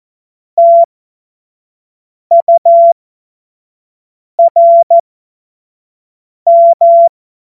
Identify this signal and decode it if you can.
Morse code, heard as a keyed tone, about 14 wpm, spaced as for 5 wpm, 680 Hz. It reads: TURM
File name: Morsekurs / Witze / Witz 18. Januar 2022, 25 F70